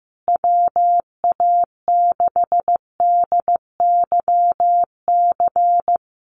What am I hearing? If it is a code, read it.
Morse code, heard as a keyed tone, about 15 wpm, 700 Hz. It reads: WA6DYC
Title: USA Call Signs Repeated 3x 15wpm